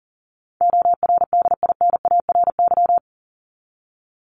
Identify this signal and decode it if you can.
Morse code, heard as a keyed tone, about 40 wpm, 700 Hz. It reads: ORDINARY
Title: Top 1751-2000 Words 40wpm